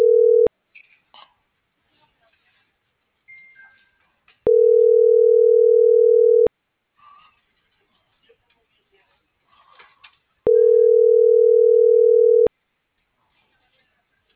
Ringback tone (North America), 440 + 480 Hz.